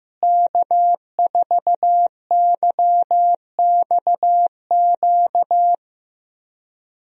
Morse code, heard as a keyed tone, about 15 wpm, 700 Hz. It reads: K4YXQ
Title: USA Call Signs - 5 Characters - Repeated 3x - 15wpm